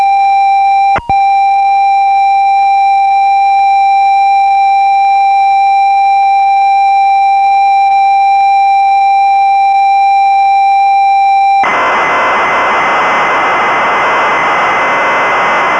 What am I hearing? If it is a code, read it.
Computer data on cassette.